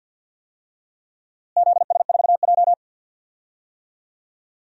Morse code, heard as a keyed tone, about 48 wpm, 700 Hz. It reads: ZS4J
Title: DXCC-100 Call Signs - 4 Characters - Repeated 3x - 50wpm